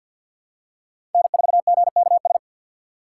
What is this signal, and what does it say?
Morse code, heard as a keyed tone, about 53 wpm, 700 Hz. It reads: N4GKS
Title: USA Call Signs - 4x Character Spacing - 50wpm